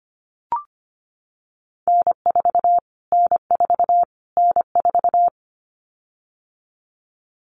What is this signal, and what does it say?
Morse code, heard as a keyed tone, about 25 wpm, 700 Hz. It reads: N4 N4 N4
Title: Callsign Prefixes for Top 25 DXCC - Repeated 3x - 25wpm